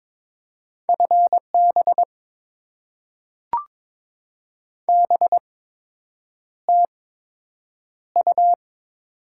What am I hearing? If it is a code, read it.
Morse code, heard as a keyed tone, about 22 wpm, spaced as for 8 wpm, 700 Hz. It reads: FB BTU
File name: Top 20 QSO Elements - 8x Character Spacing - 22wpm